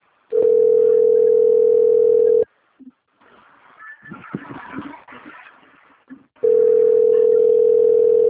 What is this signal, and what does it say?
Ringback tone (North America), 440 + 480 Hz.